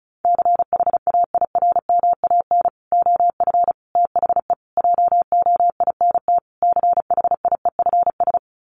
Morse code, heard as a keyed tone, about 34 wpm, 700 Hz. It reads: CHAIRMAN OF THE JOINT CHIEFS
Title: Sets of 5 Words Encore 35wpm